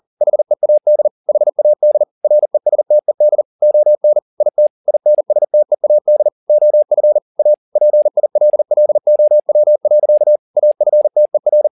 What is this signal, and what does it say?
Morse code, heard as a keyed tone, about 40 wpm, 600 Hz. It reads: HEAD HAD RESTED ON IT INSTEAD OF A PILLOW. AFTER
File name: the old man and the sea 2 40wpm